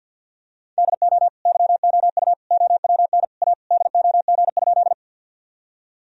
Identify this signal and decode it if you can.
Morse code, heard as a keyed tone, about 50 wpm, 700 Hz. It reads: DO YOU OWN A DOG?
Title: Sentences from Top 1000 Words - Review Part 2 - 50wpm